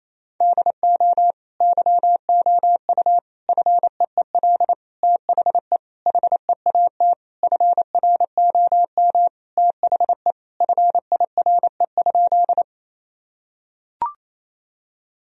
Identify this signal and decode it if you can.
Morse code, heard as a keyed tone, about 28 wpm, 700 Hz. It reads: DO YOU FEEL THE HEAT FROM THE FIRE?
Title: Sentences from Top 1000 Words - Review Part 3 - 28wpm